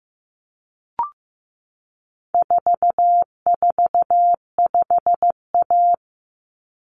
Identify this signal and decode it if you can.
Morse code, heard as a keyed tone, about 15 wpm, 700 Hz. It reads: 445A
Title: Common Cut Numbers - 4-Digit 15wpm